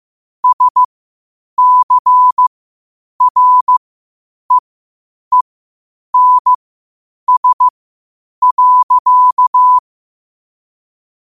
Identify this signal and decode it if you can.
Morse code, heard as a keyed tone, about 15 wpm, spaced as for 8 wpm, 1000 Hz. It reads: SCREENS.
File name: qotdfiles / 8wpm